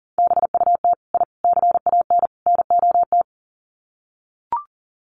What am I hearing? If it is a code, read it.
Morse code, heard as a keyed tone, about 40 wpm, 700 Hz. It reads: BUT I CAN NOT
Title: Sets of 4 Words 40wpm